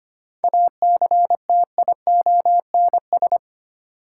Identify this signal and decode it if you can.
Morse code, heard as a keyed tone, about 25 wpm, 700 Hz. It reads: ACTIONS
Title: Sentences from Top 1000 Words - Review Part 6 - New Word Drill - 25wpm